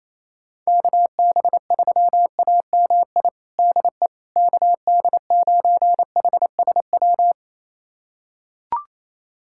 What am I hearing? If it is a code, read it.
Morse code, heard as a keyed tone, about 28 wpm, 700 Hz. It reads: KB3AMI DE KD9HSW
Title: USA Callsign DE Callsign - 6 Characters - 28wpm